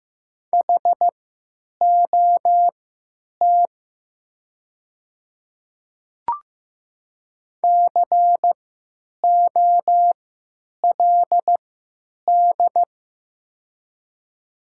Morse code, heard as a keyed tone, about 15 wpm, spaced as for 9 wpm, 700 Hz. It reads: HOT COLD
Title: 3x Character Spacing - Top 500 Words - 15wpm